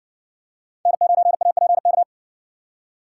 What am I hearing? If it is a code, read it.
Morse code, heard as a keyed tone, about 57 wpm, 700 Hz. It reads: N9APK